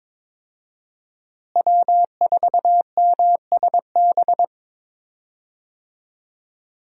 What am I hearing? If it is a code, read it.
Morse code, heard as a keyed tone, about 22 wpm, 700 Hz. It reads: W4MSB
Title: USA Call Signs - 5 Characters - Repeated 2x - 22wpm